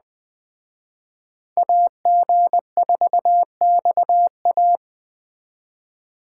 Morse code, heard as a keyed tone, about 20 wpm, 700 Hz. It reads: AG4XA
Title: Speed Racing - USA Call Signs 17wpm